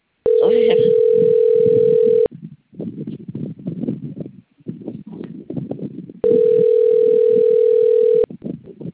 Ringback tone (North America), 440 + 480 Hz.